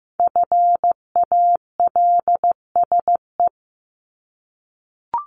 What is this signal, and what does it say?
Morse code, heard as a keyed tone, about 15 wpm, 700 Hz. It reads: FALSE
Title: Top 1501-1750 Words 15wpm